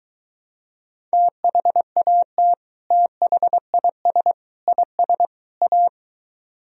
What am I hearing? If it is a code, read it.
Morse code, heard as a keyed tone, about 23 wpm, 700 Hz. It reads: THAT THIS IS A